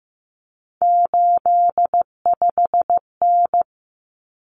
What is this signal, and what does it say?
Morse code, heard as a keyed tone, about 15 wpm, 700 Hz. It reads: 85N
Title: Common Cut Numbers - 3-Digit 15wpm